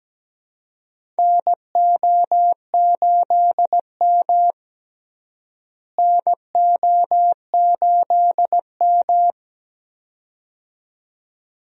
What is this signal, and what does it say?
Morse code, heard as a keyed tone, about 17 wpm, 700 Hz. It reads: NO8M NO8M
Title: USA Call Signs - 4 Characters - Repeated 2x - 17wpm